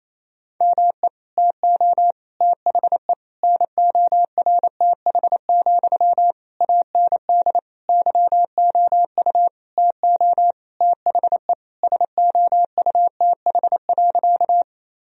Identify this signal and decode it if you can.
Morse code, heard as a keyed tone, about 28 wpm, 700 Hz. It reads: ME TO THE NORTH, AND YOU TO THE SOUTH.
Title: Sentences from Top 1000 Words - Review Part 2 - 28wpm